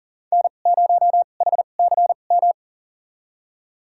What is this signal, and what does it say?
Morse code, heard as a keyed tone, about 38 wpm, spaced as for 28 wpm, 700 Hz. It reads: N0HCM
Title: USA Call Signs - 5 Characters - 2x Character Spacing - 40wpm